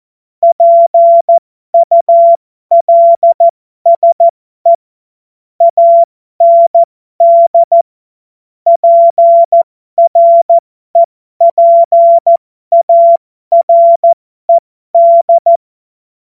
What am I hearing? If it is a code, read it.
Morse code, heard as a keyed tone, about 14 wpm, spaced as for 12 wpm, 680 Hz. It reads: PULSE AND PREPARED